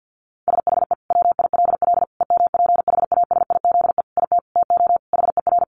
Morse code, heard as a keyed tone, about 50 wpm, 700 Hz. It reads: SHE WILL EMPHASIZE IT TO HER
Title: Sentences from Top 1500 Words 50wpm